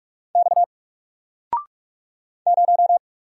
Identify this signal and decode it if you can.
Morse code, heard as a keyed tone, about 44 wpm, 700 Hz. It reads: X 0
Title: Single Character (Letter:Number:Punctuation:Prosign) 45wpm